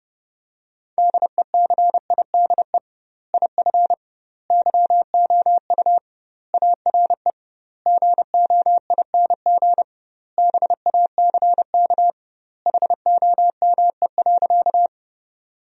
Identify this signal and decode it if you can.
Morse code, heard as a keyed tone, about 29 wpm, 700 Hz. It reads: DECIDE IF YOU ARE GOING BACK HOME.
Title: 2x Word-Spacing - Sentences from Top 500 Words - 30wpm